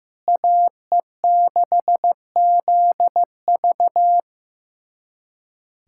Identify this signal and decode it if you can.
Morse code, heard as a keyed tone, about 16 wpm, 700 Hz. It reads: AE6ZV
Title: USA Callsign DE Callsign - 5 Characters - 15wpm